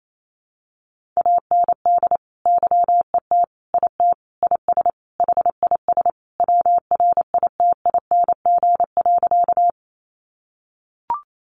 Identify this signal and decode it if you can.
Morse code, heard as a keyed tone, about 28 wpm, 700 Hz. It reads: AND YET IT IS HIS WRITING.
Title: Sentences from Top 800 Words 28wpm